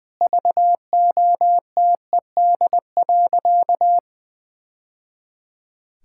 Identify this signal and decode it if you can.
Morse code, heard as a keyed tone, about 21 wpm, 700 Hz. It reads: VOTED.
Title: Sentences with 3-Word Phrases - 20wpm